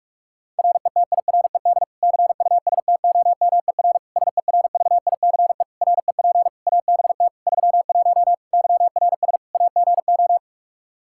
Morse code, heard as a keyed tone, about 46 wpm, 700 Hz. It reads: RETIRED CUSTOMER SERVICE REP ABT 31 YRS AGO
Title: Practice QSO - Exchange 3 - More Info - 45wpm